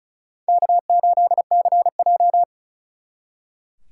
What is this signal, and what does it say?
Morse code, heard as a keyed tone, about 46 wpm, 700 Hz. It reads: K8CJ